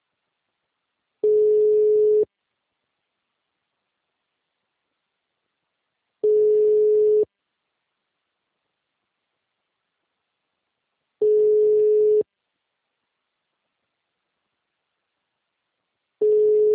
Ringback tone (Europe, 425 Hz).